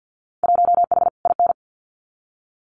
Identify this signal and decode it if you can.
Morse code, heard as a keyed tone, about 50 wpm, 700 Hz. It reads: 1H IN
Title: Winter Field Day Exchanges 50wpm